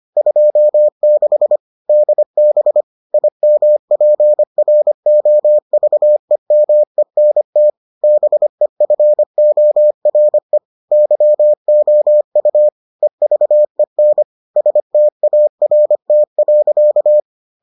Morse code, heard as a keyed tone, about 25 wpm, 600 Hz. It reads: 26 DB IMPROVEMENT BEFORE YOU EVEN START.